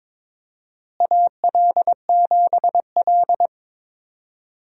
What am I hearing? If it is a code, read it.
Morse code, heard as a keyed tone, about 22 wpm, 700 Hz. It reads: AL7L
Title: USA Call Signs - 4 Characters - Repeated 3x - 22wpm